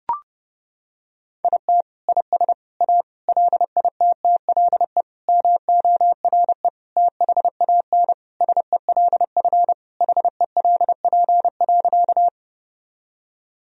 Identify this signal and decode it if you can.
Morse code, heard as a keyed tone, about 30 wpm, 700 Hz. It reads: IT IS A LITTLE MORE THAN SELF HELP.
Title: Sentences from Top 1000 Words - Review Part 1 - 30wpm